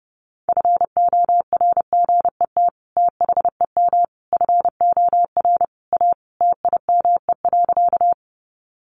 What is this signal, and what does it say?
Morse code, heard as a keyed tone, about 30 wpm, 700 Hz. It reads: FORGET THEM FOR A TIME.
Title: Sentences from Top 1200 Words 30wpm